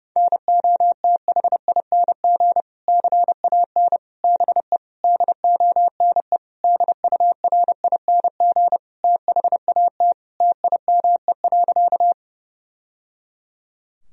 Morse code, heard as a keyed tone, about 30 wpm, 700 Hz. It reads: NOTHING CAN BE DONE DURING THAT TIME.
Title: Sentences from Top 1000 Words - Review Part 2 - 30wpm